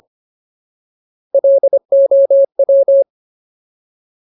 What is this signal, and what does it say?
Morse code, heard as a keyed tone, about 25 wpm, 550 Hz. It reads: LOW
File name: Ultimate Word List / 25WPM Ultimate